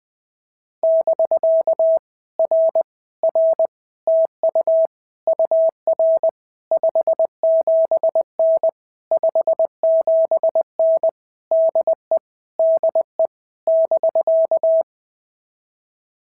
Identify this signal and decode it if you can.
Morse code, heard as a keyed tone, about 19 wpm, 650 Hz. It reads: <BK> R R TU UR 57N 57N DE DE <BK>
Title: POTA QSO Practice - 20wpm